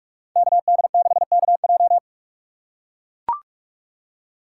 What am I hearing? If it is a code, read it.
Morse code, heard as a keyed tone, about 44 wpm, 700 Hz. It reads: KD6KJ